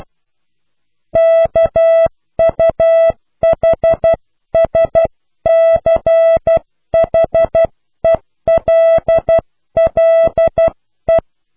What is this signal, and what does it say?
Morse code, heard as a keyed tone, about 12 wpm, 660 Hz. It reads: KUHSCHELLE
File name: cw / pseudo worden(10)